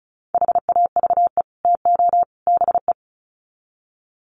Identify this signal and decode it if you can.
Morse code, heard as a keyed tone, about 34 wpm, 700 Hz. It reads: HAVE TO BE
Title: Sets of 3 Words 35wpm